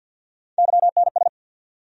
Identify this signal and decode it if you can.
Morse code, heard as a keyed tone, about 51 wpm, 700 Hz. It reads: YNS
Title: USA Call Sign Postfixes - 3 Letters 50wpm